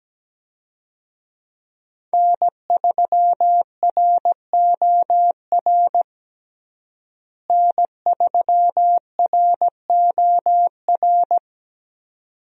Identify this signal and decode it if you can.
Morse code, heard as a keyed tone, about 17 wpm, 700 Hz. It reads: N3ROR N3ROR
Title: USA Call Signs - 5 Characters - Repeated 2x - 17wpm